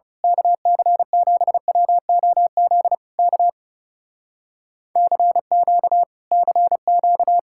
Morse code, heard as a keyed tone, about 28 wpm, 700 Hz. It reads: KC7WOZ K CQ CQ
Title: Speed-Racing Common QSO Phrases - 30wpm